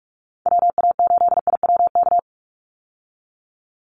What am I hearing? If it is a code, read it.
Morse code, heard as a keyed tone, about 44 wpm, 700 Hz. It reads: WA8IWK
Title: Call Signs Repeated 2x - 45wpm